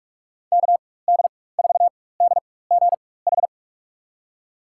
Morse code, heard as a keyed tone, about 46 wpm, spaced as for 21 wpm, 700 Hz. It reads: KD4DGH
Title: USA Call Signs - 6 Characters - 4x Character Spacing - 45wpm